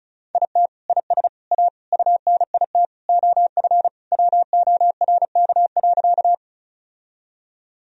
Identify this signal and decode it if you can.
Morse code, heard as a keyed tone, about 34 wpm, 700 Hz. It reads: IT IS A UNIT OF WORK.
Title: Sentences from Top 1000 Words - Review Part 2 - 35wpm